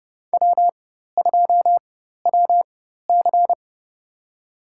Morse code, heard as a keyed tone, about 31 wpm, spaced as for 14 wpm, 700 Hz. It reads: W2WC
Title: USA Call Signs - 4 Characters - 4x Character Spacing - 30wpm